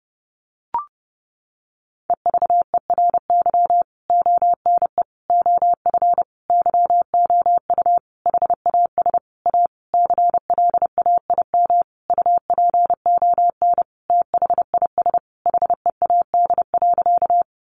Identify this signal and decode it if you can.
Morse code, heard as a keyed tone, about 30 wpm, 700 Hz. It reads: EVERY ONE OF YOU HAS A CLAIM UPON THIS HEAD.
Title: Sentences from Top 1500 Words 30wpm